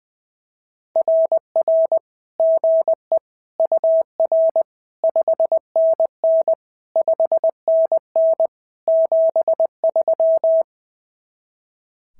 Morse code, heard as a keyed tone, about 21 wpm, 690 Hz. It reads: RR GE UR 5NN 5NN 73
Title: SOTA QSO Practice 20wpm